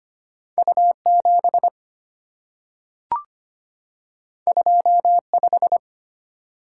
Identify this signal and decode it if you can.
Morse code, heard as a keyed tone, about 25 wpm, 700 Hz. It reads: U7 25
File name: Two Letter-Number Combinations - 25wpm